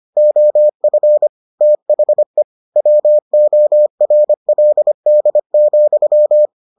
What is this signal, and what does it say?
Morse code, heard as a keyed tone, about 25 wpm, 600 Hz. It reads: OF THE WORLD,